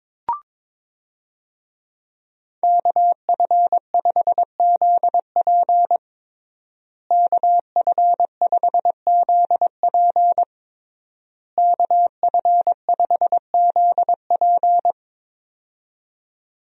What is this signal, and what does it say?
Morse code, heard as a keyed tone, about 22 wpm, 700 Hz. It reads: KF5ZP KF5ZP KF5ZP
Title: USA Call Signs - 5 Characters - Repeated 3x - 22wpm